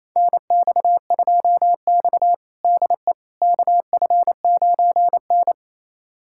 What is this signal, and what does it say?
Morse code, heard as a keyed tone, about 28 wpm, 700 Hz. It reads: NX2X DE KF9N
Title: USA Callsign DE Callsign - 4 Characters - 28wpm